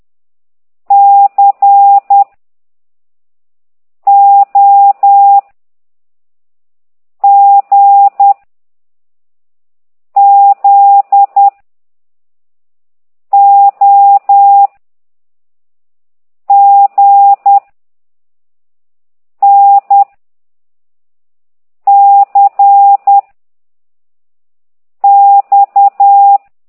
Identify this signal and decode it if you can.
Morse code, heard as a keyed tone, about 10 wpm, 800 Hz. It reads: C O G Z O G N C X